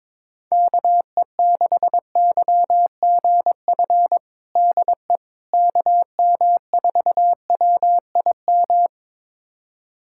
Morse code, heard as a keyed tone, about 22 wpm, 700 Hz. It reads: KE6YGF DE KM4WIM
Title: USA Callsign DE Callsign - 6 Characters - 22wpm